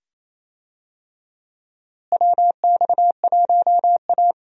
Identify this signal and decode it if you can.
Morse code, heard as a keyed tone, about 28 wpm, 700 Hz. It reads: WX1A
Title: USA Call Signs - 4 Characters - Repeated 2x - 28wpm